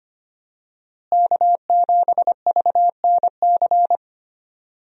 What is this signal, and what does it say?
Morse code, heard as a keyed tone, about 25 wpm, 700 Hz. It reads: K7VNC
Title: USA Call Signs - 5 Characters - Repeated 3x - 25wpm